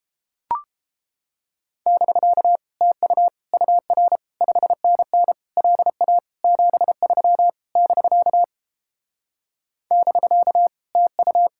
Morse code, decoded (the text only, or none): <BK> TU UR 5NN LA 73 <BK> <BK> TU